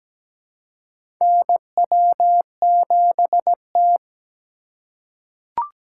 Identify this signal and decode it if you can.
Morse code, heard as a keyed tone, about 17 wpm, 700 Hz. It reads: NW7T
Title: USA Call Signs - 2x Character Spacing - 17wpm